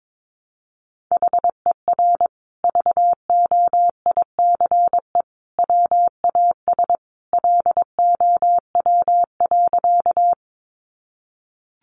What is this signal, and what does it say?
Morse code, heard as a keyed tone, about 22 wpm, 700 Hz. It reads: HER VOICE WAS LOW.
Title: Sentences from Top 500 Words 22wpm